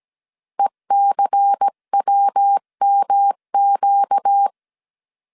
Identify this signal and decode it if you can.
Morse code, heard as a keyed tone, about 17 wpm, 800 Hz. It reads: ECWMQ
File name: CW / DET / 1 6 1500C S C 16wpm 2733